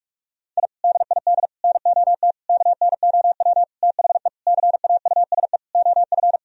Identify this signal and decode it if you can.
Morse code, heard as a keyed tone, about 42 wpm, 700 Hz. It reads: I DID NOT KNOW THE CAUSE OF
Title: Sentences from Top 1000 Words - Review Part 1 - 45wpm